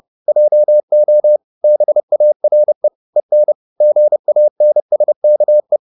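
Morse code, heard as a keyed tone, about 30 wpm, 600 Hz. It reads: JO BARE EN GANSKE
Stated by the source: High-speed Morse (QRQ) at 30 wpm.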